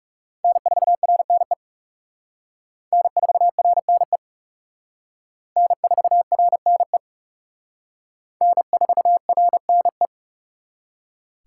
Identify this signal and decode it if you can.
Morse code, heard as a keyed tone, about 37 wpm, 700 Hz. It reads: N4RNE N4RNE N4RNE N4RNE
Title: Speed Racing - USA 1x3 Call Signs 30wpm